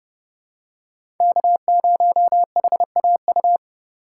Morse code, heard as a keyed tone, about 30 wpm, 700 Hz. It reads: K0HAU
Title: USA Call Signs - 3x Character Spacing - 30wpm